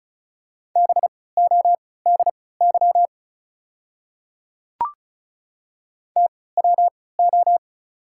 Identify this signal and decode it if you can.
Morse code, heard as a keyed tone, about 36 wpm, spaced as for 20 wpm, 690 Hz. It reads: BODY TWO
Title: 3x Character Spacing - Top 500 Words - 35wpm